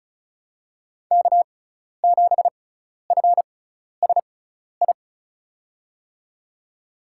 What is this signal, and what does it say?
Morse code, heard as a keyed tone, about 36 wpm, spaced as for 12 wpm, 700 Hz. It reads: K7FSI
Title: USA Call Signs - 5 Characters - 6x Character Spacing - 35wpm